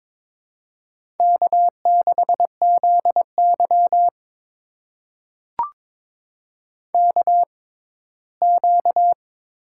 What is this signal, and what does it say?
Morse code, heard as a keyed tone, about 22 wpm, spaced as for 15 wpm, 700 Hz. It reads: K6ZY KQ
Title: USA Call Signs - 4 Characters - 6x Character Spacing - 22wpm